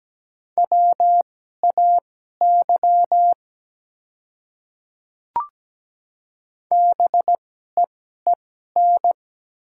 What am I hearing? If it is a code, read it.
Morse code, heard as a keyed tone, about 17 wpm, spaced as for 12 wpm, 700 Hz. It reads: WAY BEEN
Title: 2x Character Spacing - Top 100 Words - 17wpm